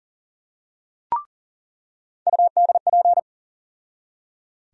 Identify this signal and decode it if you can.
Morse code, heard as a keyed tone, about 37 wpm, 700 Hz. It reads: UDP
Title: USA Call Sign Postfixes - 3 Letters 40wpm